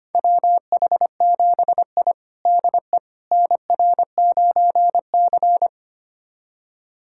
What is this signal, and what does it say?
Morse code, heard as a keyed tone, about 25 wpm, 700 Hz. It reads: WH7I DE NR9C
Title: USA Callsign DE Callsign - 4 Characters - 25wpm